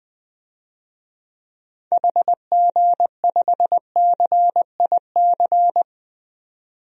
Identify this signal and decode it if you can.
Morse code, heard as a keyed tone, about 20 wpm, 700 Hz. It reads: HG5CIC
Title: DXCC-100 Call Signs - 6 Characters - 20wpm